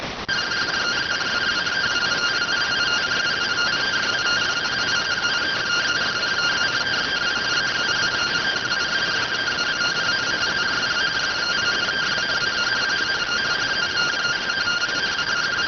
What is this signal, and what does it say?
Two-tone teleprinter signal, about 170 Hz shift.